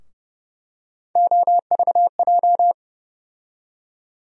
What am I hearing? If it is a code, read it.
Morse code, heard as a keyed tone, about 30 wpm, 700 Hz. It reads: OVJ
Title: USA Call Sign Postfixes - 3 Letters 30wpm